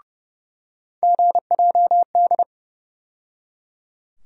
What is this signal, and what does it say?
Morse code, heard as a keyed tone, about 29 wpm, 700 Hz. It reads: GJD
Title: USA Call Sign Postfixes - 3 Letters 30wpm